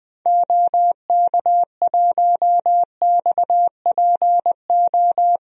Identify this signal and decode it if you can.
Morse code, heard as a keyed tone, about 21 wpm, 700 Hz. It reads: OK1XPO
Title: DXCC-100 Call Signs - 6 Characters - Repeated 2x - 20wpm